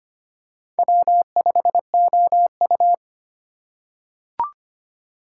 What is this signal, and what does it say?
Morse code, heard as a keyed tone, about 25 wpm, 700 Hz. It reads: W5OU
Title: USA Call Signs - 6x Character Spacing - 25wpm